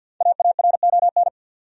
Morse code, heard as a keyed tone, about 51 wpm, 700 Hz. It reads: AARON
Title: USA Common Names - Max 5 Letters - 50wpm